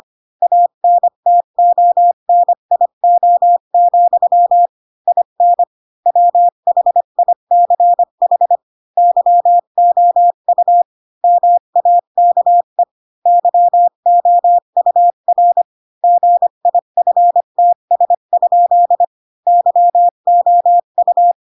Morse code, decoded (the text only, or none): ANTONIO, IN WHICH YOU MAKE YOUR GIFTS? YOU